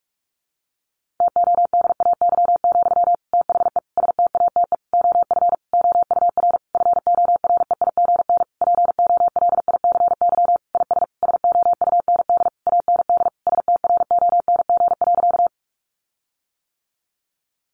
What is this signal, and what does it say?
Morse code, heard as a keyed tone, about 44 wpm, 700 Hz. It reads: TODAY, THE STATE OF OUR FOREIGN POLICY IS SOUND AND STRONG.
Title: Sentences from Top 1400 Words 45wpm